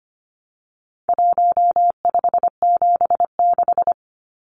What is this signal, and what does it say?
Morse code, heard as a keyed tone, about 25 wpm, 700 Hz. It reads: 1576
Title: Numbers - Contest Style - 4-Digit 25wpm